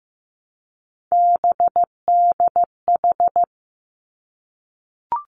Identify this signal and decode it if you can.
Morse code, heard as a keyed tone, about 15 wpm, 700 Hz. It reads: BDH
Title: Three Challenging Letter-Number Combinations 15wpm